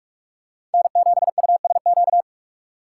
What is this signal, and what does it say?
Morse code, heard as a keyed tone, about 44 wpm, 700 Hz. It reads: N7USQ